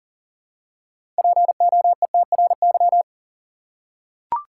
Morse code, heard as a keyed tone, about 40 wpm, 700 Hz. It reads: POETRY
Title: Six Letter Words - 40wpm